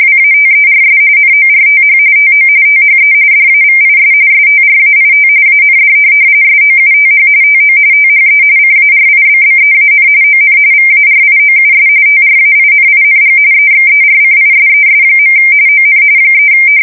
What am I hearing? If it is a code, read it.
RTTY, about 45.45 baud. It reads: WATERS NORTH TO THE DAVIS STRAIT AND SOUTH TO THE CARIBBEAN ON VARIOUS TRAINING EXERCISES. ON 3
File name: fsk170 / USS-Missouri